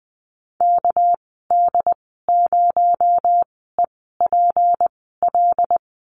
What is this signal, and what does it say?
Morse code, heard as a keyed tone, about 20 wpm, spaced as for 14 wpm, 700 Hz. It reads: KD0EPL
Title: 245 Lesson 39g - Letter z - Call Signs 20wpm